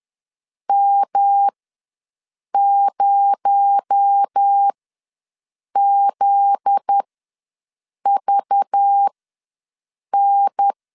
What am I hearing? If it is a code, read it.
Morse code, heard as a keyed tone, about 10 wpm, spaced as for 6 wpm, 800 Hz. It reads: M0ZVN
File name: LEZIONI / 7 GF9Z / 7 GF9Z ws 10wpm 3919